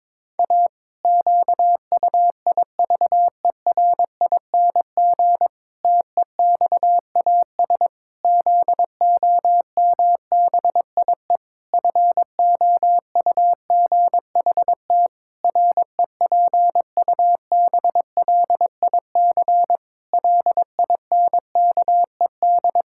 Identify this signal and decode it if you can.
Morse code, heard as a keyed tone, about 22 wpm, 700 Hz. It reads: A QUIVERING TEXAS ZOMBIE FOUGHT REPUBLIC LINKED
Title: Pangram Challenge v2 - 22wpm